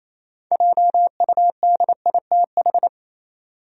Morse code, heard as a keyed tone, about 28 wpm, 700 Hz. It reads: JUDITH
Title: USA Common Names - 28wpm